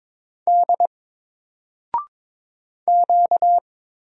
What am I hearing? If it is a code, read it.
Morse code, heard as a keyed tone, about 22 wpm, 700 Hz. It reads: D Q
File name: Single-Letter-Number-22wpm